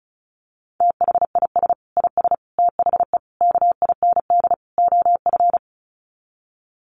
Morse code, heard as a keyed tone, about 34 wpm, 700 Hz. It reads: THIS IS THE KIND OF
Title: Sets of 5 Words Encore 35wpm